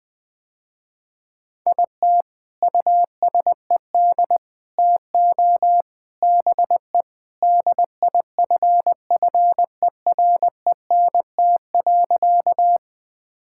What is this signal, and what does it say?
Morse code, heard as a keyed tone, about 21 wpm, 700 Hz. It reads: IT USED TO BE DIFFERENT.
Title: Sentences with 4-Word Phrases - Spoken Beginning - 20wpm